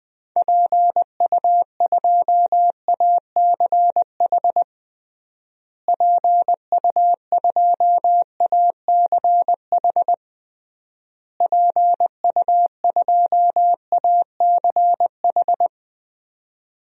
Morse code, heard as a keyed tone, about 21 wpm, 700 Hz. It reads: PU2ACH PU2ACH PU2ACH
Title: DXCC-100 Call Signs - 6 Characters - Repeated 3x - 20wpm